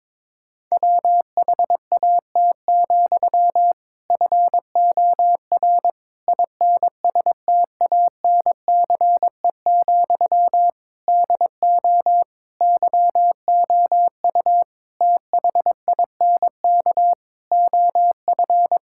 Morse code, heard as a keyed tone, about 22 wpm, 700 Hz. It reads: WHAT, FOR INSTANCE, DO YOU THINK OF
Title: Sentences from Top 1000 Words - Review Part 5 - 22wpm